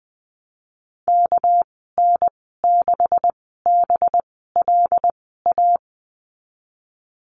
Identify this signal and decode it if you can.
Morse code, heard as a keyed tone, about 19 wpm, spaced as for 14 wpm, 700 Hz. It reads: KN6BLA
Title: 207 Lesson 33g - Letter j - Call Signs 20wpm